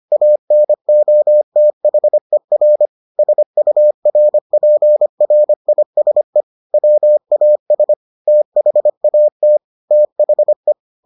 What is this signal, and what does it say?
Morse code, heard as a keyed tone, about 25 wpm, 600 Hz. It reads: ANOTHER SURPRISE WAS THAT THE